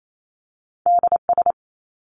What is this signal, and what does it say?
Morse code, heard as a keyed tone, about 28 wpm, 700 Hz. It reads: D S
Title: Two Challenging Letter-Number Combinations 28wpm